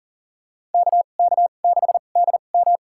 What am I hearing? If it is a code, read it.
Morse code, heard as a keyed tone, about 39 wpm, spaced as for 29 wpm, 700 Hz. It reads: KK6DM
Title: USA Call Signs - 5 Characters - 2x Character Spacing - 40wpm